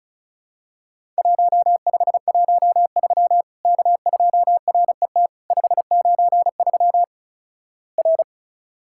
Morse code, heard as a keyed tone, about 34 wpm, 700 Hz. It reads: 1513 K2RET 593 R
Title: Worked All Europe DX Contest - QTC Exchange - 35wpm